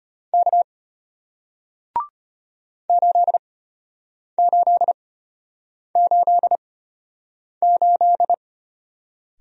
Morse code, heard as a keyed tone, about 33 wpm, spaced as for 8 wpm, 700 Hz. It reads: K 8888